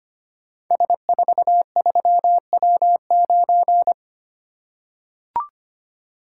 Morse code, heard as a keyed tone, about 25 wpm, 700 Hz. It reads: S43W9
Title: Code Groups - 5 Letter:Numbers - 25wpm